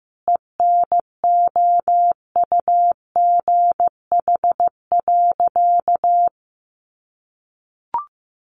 Morse code, heard as a keyed tone, about 14 wpm, 700 Hz. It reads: ENOUGH.
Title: Sentences from Top 1500 Words 15wpm